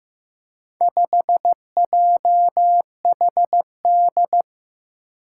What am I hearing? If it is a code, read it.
Morse code, heard as a keyed tone, about 15 wpm, 700 Hz. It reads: 5JHD
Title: Four Challenging Letter-Number Combinations 15wpm